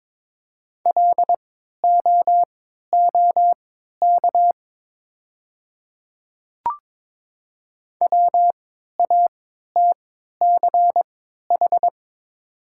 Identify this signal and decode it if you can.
Morse code, heard as a keyed tone, about 22 wpm, spaced as for 12 wpm, 700 Hz. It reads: LOOK WATCH
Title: 3x Character Spacing - Top 300 Words - 22wpm